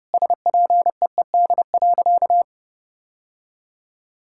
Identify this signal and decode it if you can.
Morse code, heard as a keyed tone, about 30 wpm, 700 Hz. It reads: SPEED.
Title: Sentences from Top 1000 Words - Review Part 5 - 30wpm